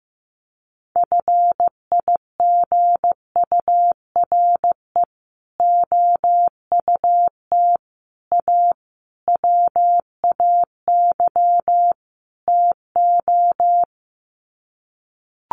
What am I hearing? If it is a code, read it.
Morse code, heard as a keyed tone, about 15 wpm, 700 Hz. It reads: FIGURE OUT A WAY TO
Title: Sets of 5 Words Encore2 15wpm